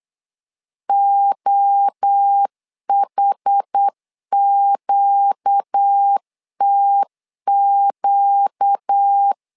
Morse code, heard as a keyed tone, about 8 wpm, 800 Hz. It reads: OHQTQ
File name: LEZIONI / 4 Q 8wpm 2732